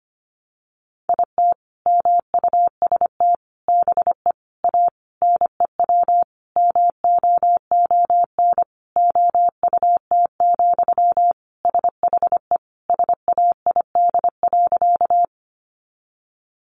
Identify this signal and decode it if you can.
Morse code, heard as a keyed tone, about 25 wpm, 700 Hz. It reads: IT MUST BE A NEW MOON OUT, SHE SAID.
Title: Sentences from Top 500 Words 25wpm